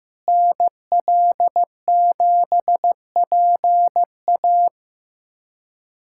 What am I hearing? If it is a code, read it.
Morse code, heard as a keyed tone, about 15 wpm, 700 Hz. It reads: NL7PA
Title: USA Callsign DE Callsign - 5 Characters - 15wpm